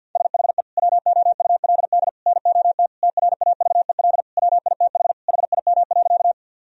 Morse code, heard as a keyed tone, about 46 wpm, 700 Hz. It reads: SHE WOULD NOT TRAVEL WITH HIM.
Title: Sentences from Top 1000 Words - Review Part 2 - 50wpm